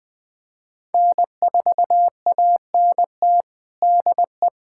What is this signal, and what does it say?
Morse code, heard as a keyed tone, about 21 wpm, 700 Hz. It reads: N4ANT DE
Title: USA Callsign DE Callsign - 5 Characters - 20wpm